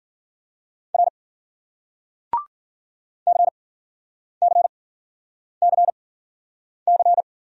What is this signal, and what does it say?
Morse code, heard as a keyed tone, about 50 wpm, spaced as for 9 wpm, 700 Hz. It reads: R CCCC